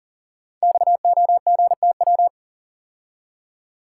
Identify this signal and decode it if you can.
Morse code, heard as a keyed tone, about 38 wpm, 690 Hz. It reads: XOGTW
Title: Code Groups - 5 Letter:Numbers - 40wpm